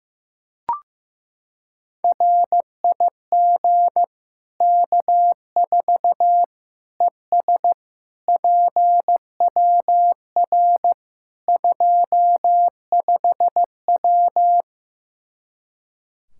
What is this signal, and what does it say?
Morse code, heard as a keyed tone, about 15 wpm, 700 Hz. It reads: RIG K4 ES PWR 25W
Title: Practice QSO - Exchange 2 - Info - 15wpm